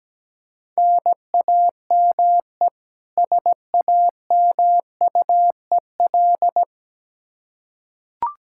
Morse code, heard as a keyed tone, about 17 wpm, 700 Hz. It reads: NAME SAMUEL
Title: Practice QSO - Exchange 1 - Beginning - 17wpm